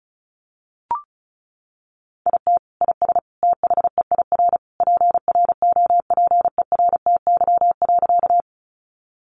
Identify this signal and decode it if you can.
Morse code, heard as a keyed tone, about 34 wpm, 700 Hz. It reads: IT IS THEIR PROPERTY.
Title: Sentences from Top 1000 Words 35wpm